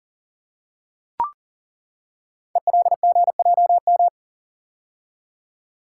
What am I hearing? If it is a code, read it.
Morse code, heard as a keyed tone, about 39 wpm, 690 Hz. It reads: ELGJM
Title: Code Groups - 5 Letters - 40wpm